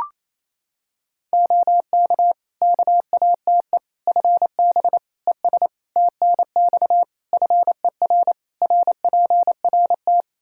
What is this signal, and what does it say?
Morse code, heard as a keyed tone, about 27 wpm, 700 Hz. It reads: OK KATE FB ES TNX FER RPRT
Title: Practice QSO - Exchange 2 - Info - 28wpm